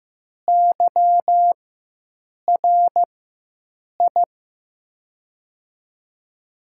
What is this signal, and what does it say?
Morse code, heard as a keyed tone, about 15 wpm, spaced as for 7 wpm, 700 Hz. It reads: YRI
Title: USA Call Signs - 4x Character Spacing - 15wpm